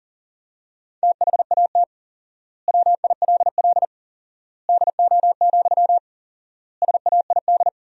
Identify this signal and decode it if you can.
Morse code, heard as a keyed tone, about 40 wpm, 700 Hz. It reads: THAT WILL DO, SAID
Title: 4x Word-Spacing - Sentences from Top 100 Words - 40wpm